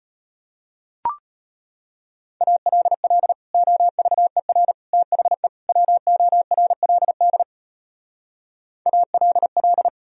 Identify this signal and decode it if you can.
Morse code, heard as a keyed tone, about 33 wpm, 700 Hz. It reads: ALL OVER THE WORLD ALL